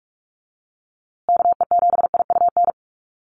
Morse code, heard as a keyed tone, about 44 wpm, 700 Hz. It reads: KE7IUN
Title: USA Call Signs Repeated 3x 45wpm